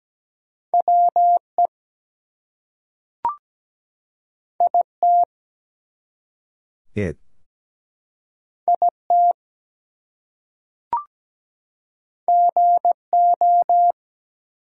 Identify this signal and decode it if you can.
Morse code, heard as a keyed tone, about 17 wpm, 700 Hz. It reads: WE IT IT GO
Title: Two Letter Words - 17wpm